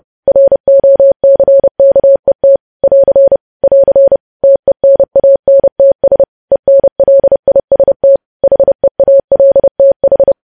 Morse code, heard as a keyed tone, about 30 wpm, 550 Hz. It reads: ROCKET <AR> <AR> TENANTS ENLIST HEALTH
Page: news headlines in morse code at 30 wpm